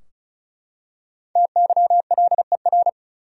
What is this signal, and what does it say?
Morse code, heard as a keyed tone, about 34 wpm, 700 Hz. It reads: TYLER
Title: USA Common Names - 35wpm